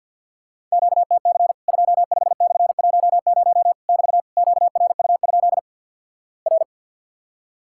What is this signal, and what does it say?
Morse code, heard as a keyed tone, about 50 wpm, 700 Hz. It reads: QTC 25/10 <BT> QRU? R
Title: Worked All Europe DX Contest - QTC Exchange - 50wpm